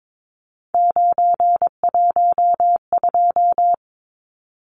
Morse code, heard as a keyed tone, about 22 wpm, 700 Hz. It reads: 912
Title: Numbers - 3-Digit 22wpm